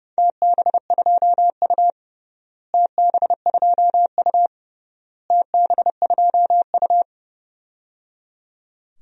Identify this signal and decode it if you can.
Morse code, heard as a keyed tone, about 30 wpm, 700 Hz. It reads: TB2U TB2U TB2U
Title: DXCC-100 Call Signs - 4 Characters - Repeated 3x - 30wpm